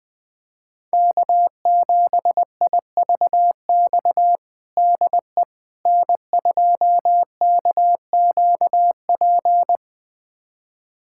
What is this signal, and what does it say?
Morse code, heard as a keyed tone, about 20 wpm, 700 Hz. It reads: K7IVX DE N2KQP
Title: USA Callsign DE Callsign - 5 Characters - 20wpm